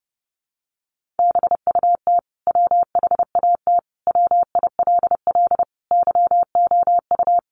Morse code, heard as a keyed tone, about 30 wpm, 700 Hz. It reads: BUT WHAT WILL YOU
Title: Sentences from Top 100 Words 30wpm